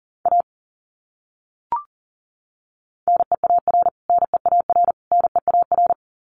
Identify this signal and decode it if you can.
Morse code, heard as a keyed tone, about 40 wpm, 700 Hz. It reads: A NEAR NEAR NEAR
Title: Top 200 Words Repeated 40wpm